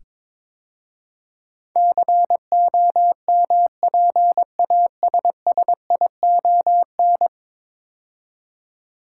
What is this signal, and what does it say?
Morse code, heard as a keyed tone, about 22 wpm, 700 Hz. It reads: COMPASSION
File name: practice / 22wpm / 10 Letter Words - First 3 Letters Given - 22wpm